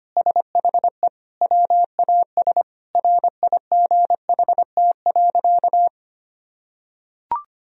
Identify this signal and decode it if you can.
Morse code, heard as a keyed tone, about 25 wpm, 700 Hz. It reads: SHE WAS RIGHT.
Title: Sentences from Top 1000 Words - Review Part 1 - 25wpm